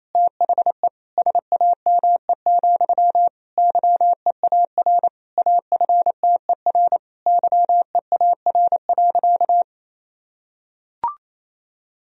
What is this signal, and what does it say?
Morse code, heard as a keyed tone, about 27 wpm, 700 Hz. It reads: THE SAME, YEAR AFTER YEAR.
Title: Sentences from Top 1000 Words - Review Part 1 - 28wpm